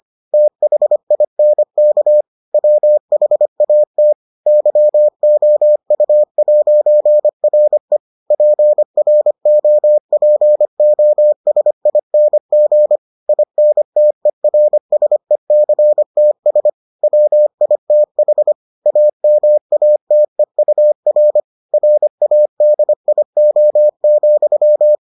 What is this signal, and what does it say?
Morse code, heard as a keyed tone, about 25 wpm, 600 Hz. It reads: THINK WHAT YOU'RE PROPOSING INTERSECTS WITH AMATEUR RADIO,